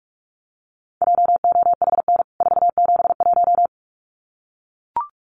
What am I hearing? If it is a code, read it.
Morse code, heard as a keyed tone, about 44 wpm, 700 Hz. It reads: JOHN 471
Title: CWT Exchange 45wpm